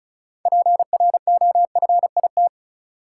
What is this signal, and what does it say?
Morse code, heard as a keyed tone, about 34 wpm, 700 Hz. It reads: PROFIT
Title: Sentences from Top 800 Words - New Word Drill - 35wpm